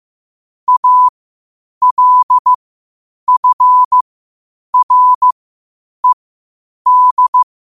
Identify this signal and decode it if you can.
Morse code, heard as a keyed tone, about 15 wpm, spaced as for 8 wpm, 1000 Hz. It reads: ALFRED
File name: qotdfiles / 8wpm